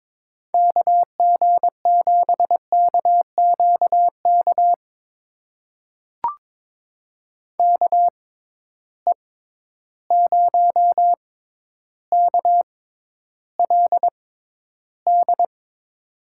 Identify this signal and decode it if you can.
Morse code, heard as a keyed tone, about 22 wpm, spaced as for 11 wpm, 700 Hz. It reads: KG7KQK KE0KLD
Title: USA Call Signs - 6 Characters - 6x Character Spacing - 22wpm